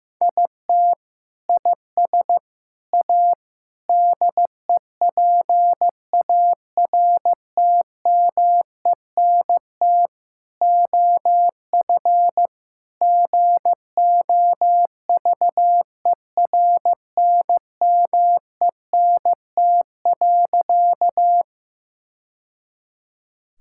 Morse code, heard as a keyed tone, about 15 wpm, 700 Hz. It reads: IT IS A DEPARTMENT OF GOVERNMENT.
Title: Sentences from Top 1000 Words - Review Part 5 - 15wpm